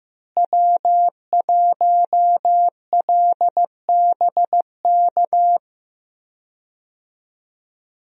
Morse code, heard as a keyed tone, about 15 wpm, 700 Hz. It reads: W1LBK
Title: USA Call Signs - 5 Characters - Repeated 3x - 15wpm